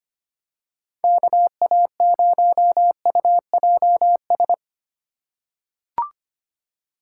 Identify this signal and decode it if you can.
Morse code, heard as a keyed tone, about 25 wpm, 700 Hz. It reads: KA0UJS
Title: USA Call Signs - 4x Character Spacing - 25wpm